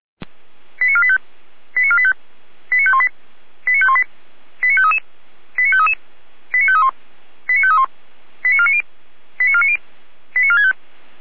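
Five-tone selective call (ZVEI), heard as quick runs of five tones, tones of about 70 ms. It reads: 89485 89485 89518 89518 89533 89533 89531 89531 89490 89490 89465
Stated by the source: ZVEI five-tone selective call received by SDR.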